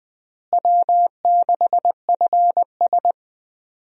Morse code, heard as a keyed tone, about 20 wpm, 700 Hz. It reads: W6FS
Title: USA Call Signs - 4 Characters - Repeated 2x - 20wpm